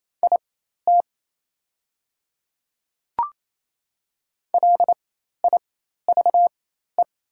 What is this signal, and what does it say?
Morse code, heard as a keyed tone, about 28 wpm, spaced as for 13 wpm, 700 Hz. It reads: IT LIVE
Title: 4x Character Spacing - Top 200 Words - 28wpm